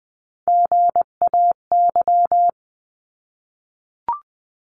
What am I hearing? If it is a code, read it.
Morse code, heard as a keyed tone, about 20 wpm, 700 Hz. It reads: GAY
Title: 259 Lesson 42d - Words 18-20wpm